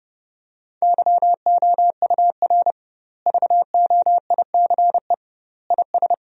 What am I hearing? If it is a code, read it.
Morse code, heard as a keyed tone, about 30 wpm, 700 Hz. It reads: YOUR VOICE IS
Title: 2x Word-Spacing - Sentences from Top 500 Words - 30wpm